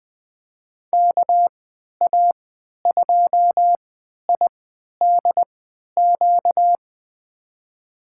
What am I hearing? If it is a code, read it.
Morse code, heard as a keyed tone, about 20 wpm, spaced as for 11 wpm, 700 Hz. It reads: KA2IDQ
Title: USA Call Signs - 6 Characters - 3x Character Spacing - 20wpm